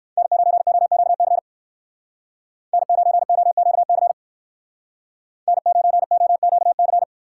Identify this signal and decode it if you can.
Morse code, heard as a keyed tone, about 59 wpm, 700 Hz. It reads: N9OQC N9OQC N9OQC